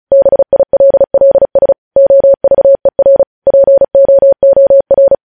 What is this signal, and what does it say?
Morse code, heard as a keyed tone, about 36 wpm, 550 Hz. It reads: BILLS OVER POOR